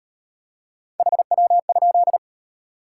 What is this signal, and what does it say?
Morse code, heard as a keyed tone, about 36 wpm, 700 Hz. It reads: HW?